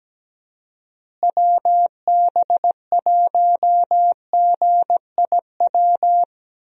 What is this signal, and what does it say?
Morse code, heard as a keyed tone, about 17 wpm, 700 Hz. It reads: WB1GIW
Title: USA Call Signs - 6 Characters - Repeated 3x - 17wpm